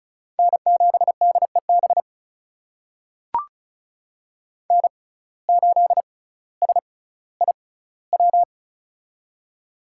Morse code, heard as a keyed tone, about 36 wpm, spaced as for 18 wpm, 700 Hz. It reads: N7DEB N8SIW
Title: USA Call Signs - 5 Characters - 6x Character Spacing - 35wpm